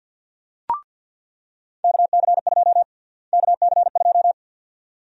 Morse code, heard as a keyed tone, about 47 wpm, 700 Hz. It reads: KC2 KC2
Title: USA Call Sign Prefixes - Repeated 2x - 50wpm